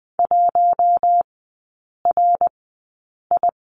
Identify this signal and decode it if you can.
Morse code, heard as a keyed tone, about 20 wpm, spaced as for 8 wpm, 700 Hz. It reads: 1RI
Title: 053 Lesson 10e - Letter r - Three Characters 20wpm